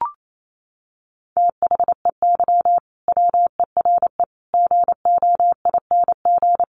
Morse code, heard as a keyed tone, about 28 wpm, 700 Hz. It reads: THEY WERE GOING
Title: Sets of 4 Words 28wpm